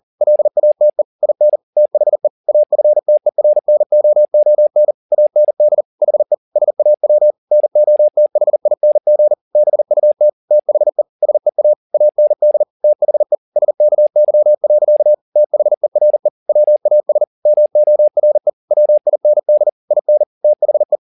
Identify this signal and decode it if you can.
Morse code, heard as a keyed tone, about 40 wpm, 600 Hz. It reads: LATE IN THE AFTERNOON AND HE SAW NOTHING BUT THE SEA AND THE SKY. THERE WAS MORE WIND IN THE
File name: the old man and the sea 2 40wpm